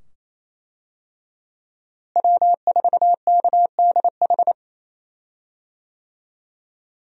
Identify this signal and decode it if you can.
Morse code, heard as a keyed tone, about 28 wpm, 700 Hz. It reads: W4KDH
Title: USA Call Signs - 5 Characters - Repeated 2x - 28wpm